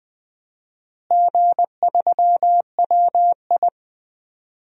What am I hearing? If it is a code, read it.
Morse code, heard as a keyed tone, about 20 wpm, 700 Hz. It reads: G3WI
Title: DXCC-100 Call Signs - 4 Characters - Repeated 2x - 20wpm